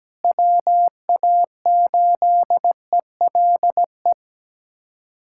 Morse code, heard as a keyed tone, about 17 wpm, 700 Hz. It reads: WA8ELE
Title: USA Callsign DE Callsign - 6 Characters - 17wpm